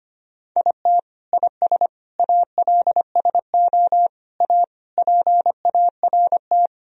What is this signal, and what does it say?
Morse code, heard as a keyed tone, about 25 wpm, 700 Hz. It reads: IT IS ALSO A PART
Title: Sentences from Top 1000 Words - Review Part 1 - 25wpm